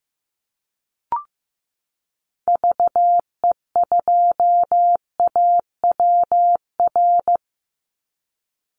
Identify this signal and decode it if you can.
Morse code, heard as a keyed tone, about 15 wpm, 700 Hz. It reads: VE2AWR
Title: Call Signs Repeated 3x - 15wpm